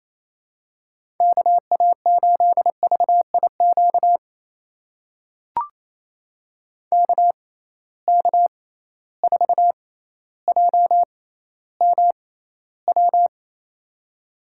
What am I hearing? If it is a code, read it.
Morse code, heard as a keyed tone, about 28 wpm, spaced as for 14 wpm, 700 Hz. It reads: KA8VIQ KK4JMW
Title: USA Call Signs - 6 Characters - 6x Character Spacing - 28wpm